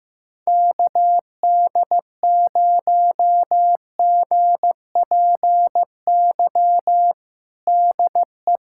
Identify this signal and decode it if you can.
Morse code, heard as a keyed tone, about 16 wpm, 700 Hz. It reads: KD0GPY DE
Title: USA Callsign DE Callsign 15wpm